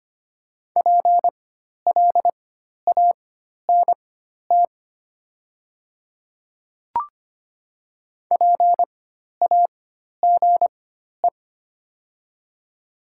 Morse code, heard as a keyed tone, about 25 wpm, spaced as for 12 wpm, 700 Hz. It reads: PLANT PAGE
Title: 4x Character Spacing - Top 300 Words - 25wpm